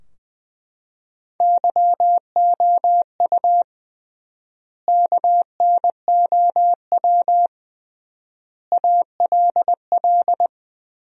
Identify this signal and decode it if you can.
Morse code, heard as a keyed tone, about 21 wpm, 700 Hz. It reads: YOU KNOW ALL
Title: 3x Word-Spacing - Sentences from Top 300 Words - 20wpm